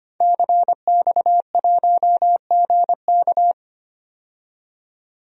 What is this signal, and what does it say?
Morse code, heard as a keyed tone, about 25 wpm, 700 Hz. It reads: CX1GK
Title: DXCC-100 Call Signs - 5 Characters - 25wpm